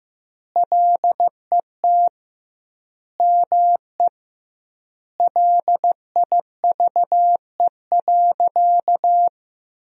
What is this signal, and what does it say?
Morse code, heard as a keyed tone, about 15 wpm, 700 Hz. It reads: LET ME LIVE.
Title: 2x Word-Spacing - Sentences from Top 300 Words - 15wpm